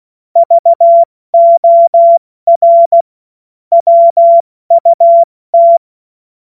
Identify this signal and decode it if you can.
Morse code, heard as a keyed tone, about 16 wpm, spaced as for 14 wpm, 680 Hz. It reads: VOR WUT
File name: Morsekurs / Witze / Witz 29. Juni 2021, 70 F80